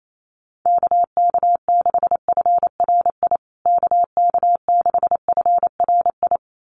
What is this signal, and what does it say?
Morse code, heard as a keyed tone, about 28 wpm, 700 Hz. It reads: KK6FRI KK6FRI
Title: USA Call Signs Repeated 2x 28wpm